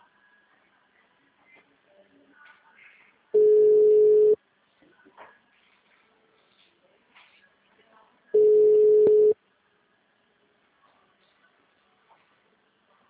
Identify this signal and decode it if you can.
Ringback tone (Europe, 425 Hz), with speech.